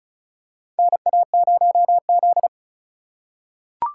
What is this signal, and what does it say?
Morse code, heard as a keyed tone, about 34 wpm, 700 Hz. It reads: NA0Z
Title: USA Call Signs - 4 Characters - 4x Character Spacing - 35wpm